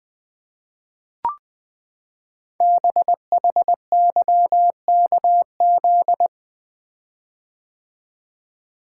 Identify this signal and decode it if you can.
Morse code, heard as a keyed tone, about 20 wpm, 700 Hz. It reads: BHYKZ
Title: Code Groups - 5 Letters - 20wpm